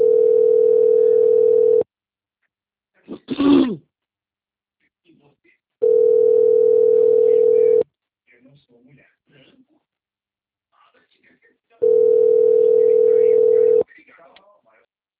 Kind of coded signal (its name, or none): telephone call progress tone